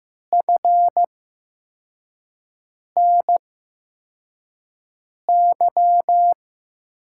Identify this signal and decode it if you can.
Morse code, heard as a keyed tone, about 15 wpm, spaced as for 4 wpm, 700 Hz. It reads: FNY
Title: USA Call Signs - 5 Characters - 8x Character Spacing - 15wpm